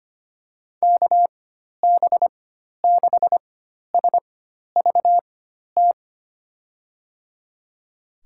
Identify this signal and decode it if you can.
Morse code, heard as a keyed tone, about 25 wpm, spaced as for 12 wpm, 700 Hz. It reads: KB6SVT
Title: USA Call Signs - 6 Characters - 4x Character Spacing - 25wpm